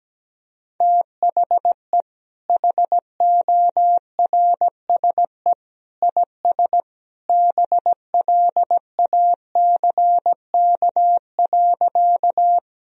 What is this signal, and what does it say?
Morse code, heard as a keyed tone, about 17 wpm, 700 Hz. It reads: THE HORSE IS BLACK.
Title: Sentences from Top 1000 Words - Review Part 2 - 17wpm